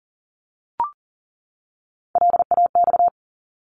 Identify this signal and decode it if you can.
Morse code, heard as a keyed tone, about 40 wpm, 700 Hz. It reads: LAX
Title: ARRL Sections 40wpm